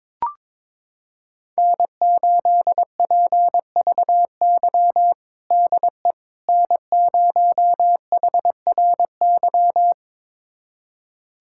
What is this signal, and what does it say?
Morse code, heard as a keyed tone, about 22 wpm, 700 Hz. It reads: N8PVY DE N0HRY
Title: USA Callsign DE Callsign - 5 Characters - 22wpm